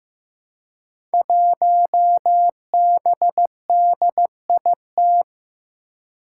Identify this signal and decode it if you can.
Morse code, heard as a keyed tone, about 15 wpm, 700 Hz. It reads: 1BDIT
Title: Five Challenging Letter-Number Combinations 15wpm